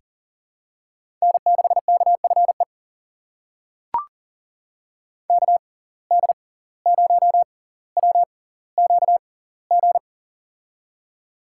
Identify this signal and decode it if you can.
Morse code, heard as a keyed tone, about 40 wpm, spaced as for 19 wpm, 700 Hz. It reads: N6KFE KD0WQG
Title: USA Call Signs - 6x Character Spacing - 40wpm